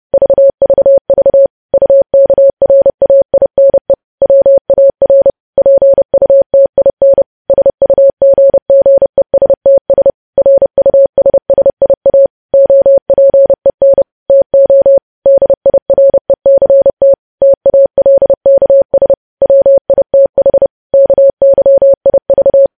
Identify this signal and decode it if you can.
Morse code, heard as a keyed tone, about 30 wpm, 550 Hz. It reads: VVV UKRAINE WAR PUTIN SUGGESTS RUSSIA OPEN TO DIRECT TALKS WITH KYIV
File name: podcast / News headlines-1x-30x30